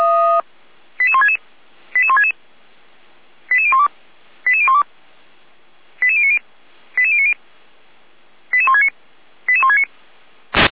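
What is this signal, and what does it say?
Five-tone selective call (ZVEI), heard as quick runs of five tones, tones of about 70 ms. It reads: 88166 88166 88012 88012 88090 88090 88169 88169
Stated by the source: ZVEI five-tone selective call received on an SDR.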